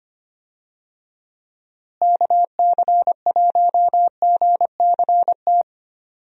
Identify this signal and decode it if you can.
Morse code, heard as a keyed tone, about 25 wpm, 700 Hz. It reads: KC1GCT
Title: USA Call Signs - 6 Characters - Repeated 3x - 25wpm